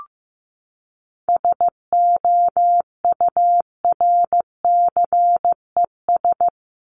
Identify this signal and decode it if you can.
Morse code, heard as a keyed tone, about 15 wpm, 700 Hz. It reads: SOURCES
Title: ES:S Suffix - Top 100 Words with Base Words 15wpm